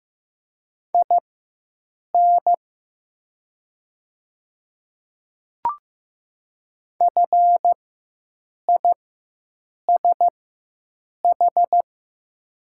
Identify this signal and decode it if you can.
Morse code, heard as a keyed tone, about 15 wpm, spaced as for 7 wpm, 700 Hz. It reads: IN FISH
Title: 4x Character Spacing - Top 300 Words - 15wpm